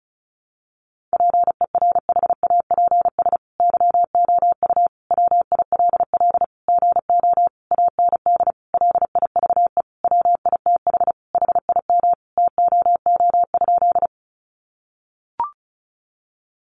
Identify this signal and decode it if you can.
Morse code, heard as a keyed tone, about 34 wpm, 700 Hz. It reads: PERHAPS YOU WILL GO AND LIVE WITH HIM TOO?
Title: Sentences from Top 500 Words 35wpm